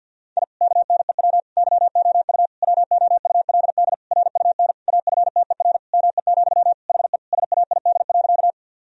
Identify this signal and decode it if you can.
Morse code, heard as a keyed tone, about 52 wpm, 700 Hz. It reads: I KNEW YOU WOULD RUN AFTER ME, HE SAID.
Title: Sentences from Top 1000 Words - Review Part 1 - 50wpm